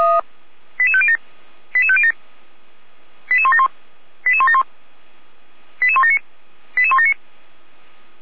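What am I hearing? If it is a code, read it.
Five-tone selective call (ZVEI), heard as quick runs of five tones, tones of about 70 ms. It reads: 88507 88507 88171 88171 88179 88179
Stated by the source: ZVEI five-tone call 88391 received on SDR.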